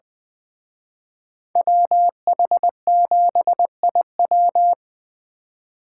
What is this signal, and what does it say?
Morse code, heard as a keyed tone, about 24 wpm, 700 Hz. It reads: WH7IW
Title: Speed Racing - USA 2x2 Call Signs 20wpm